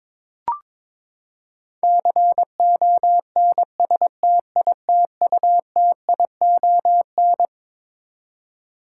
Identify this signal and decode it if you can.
Morse code, heard as a keyed tone, about 21 wpm, 700 Hz. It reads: CONSTITUTION
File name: Twelve Letter Words - 22wpm